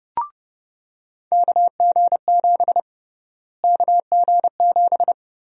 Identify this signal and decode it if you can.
Morse code, heard as a keyed tone, about 30 wpm, 700 Hz. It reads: KG7 KG7
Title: USA Call Sign Prefixes - Repeated 2x - 30wpm